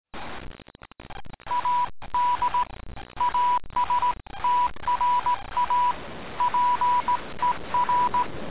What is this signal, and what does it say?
Morse code, heard as a keyed tone, about 18 wpm, 980 Hz, with noise. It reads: AD ASTRA PER